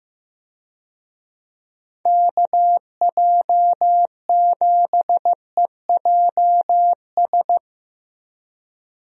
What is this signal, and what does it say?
Morse code, heard as a keyed tone, about 15 wpm, 700 Hz. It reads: KJ7EJS
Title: USA Call Signs - 6 Characters - 15wpm